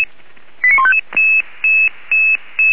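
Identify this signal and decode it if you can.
Five-tone selective call (ZVEI), heard as quick runs of five tones, tones of about 70 ms. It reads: 89155
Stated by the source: ZVEI five-tone selective call received by SDR.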